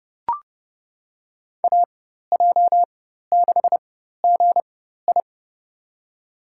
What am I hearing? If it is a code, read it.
Morse code, heard as a keyed tone, about 30 wpm, spaced as for 14 wpm, 700 Hz. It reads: AJ6GI
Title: USA Call Signs - 5 Characters - 4x Character Spacing - 30wpm